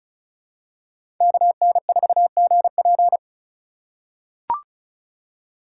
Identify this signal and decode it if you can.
Morse code, heard as a keyed tone, about 36 wpm, 700 Hz. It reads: KN4GP
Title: USA Call Signs - 4x Character Spacing - 35wpm